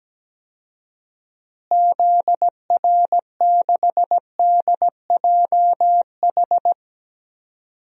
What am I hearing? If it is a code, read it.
Morse code, heard as a keyed tone, about 17 wpm, 700 Hz. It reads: ZR6DJH
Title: DXCC-100 Call Signs - 6 Characters - Repeated 2x - 17wpm